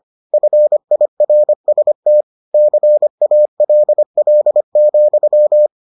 Morse code, heard as a keyed tone, about 25 wpm, 600 Hz. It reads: FIRST CALL,